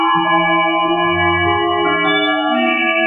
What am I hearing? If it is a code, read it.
There is no code.